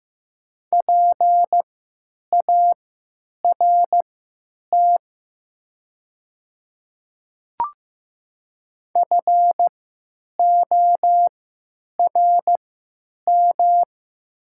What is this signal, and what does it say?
Morse code, heard as a keyed tone, about 15 wpm, spaced as for 9 wpm, 700 Hz. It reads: PART FORM
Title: 3x Character Spacing - Top 200 Words - 15wpm